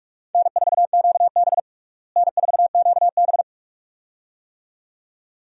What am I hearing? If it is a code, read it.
Morse code, heard as a keyed tone, about 42 wpm, 700 Hz. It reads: N4QB N4QB
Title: USA Call Signs - 4 Characters - Repeated 2x - 45wpm